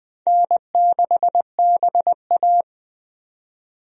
Morse code, heard as a keyed tone, about 20 wpm, 700 Hz. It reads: N6BA
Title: USA Call Signs - 4 Characters - 20wpm